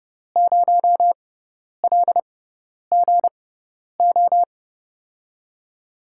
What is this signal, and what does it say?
Morse code, heard as a keyed tone, about 30 wpm, spaced as for 10 wpm, 700 Hz. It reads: 0LGO
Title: USA Call Signs - 6 Characters - 6x Character Spacing - 30wpm